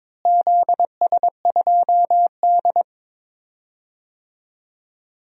Morse code, heard as a keyed tone, about 22 wpm, 700 Hz. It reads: ZS2D
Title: DXCC-100 Call Signs - 4 Characters - Repeated 2x - 22wpm